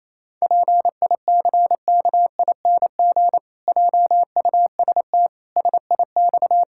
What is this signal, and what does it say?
Morse code, heard as a keyed tone, about 28 wpm, 700 Hz. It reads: PICKING JUST SIX
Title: Pangram Challenge v2 - 28wpm